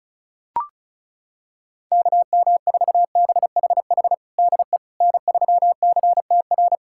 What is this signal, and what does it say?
Morse code, heard as a keyed tone, about 36 wpm, 700 Hz. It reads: KM4BHH DE N3CTR
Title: USA Callsign DE Callsign 35wpm